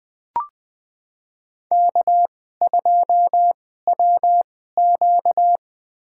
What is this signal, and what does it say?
Morse code, heard as a keyed tone, about 20 wpm, spaced as for 14 wpm, 700 Hz. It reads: K2WQ
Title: USA Call Signs - 4 Characters - 2x Character Spacing - 20wpm